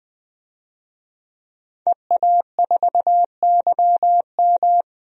Morse code, heard as a keyed tone, about 20 wpm, 700 Hz. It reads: EA4YM
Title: DXCC-100 Call Signs - 5 Characters - 20wpm